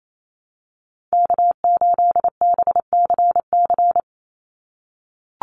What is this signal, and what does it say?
Morse code, heard as a keyed tone, about 28 wpm, 700 Hz. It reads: K8BCC
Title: USA Call Signs 28wpm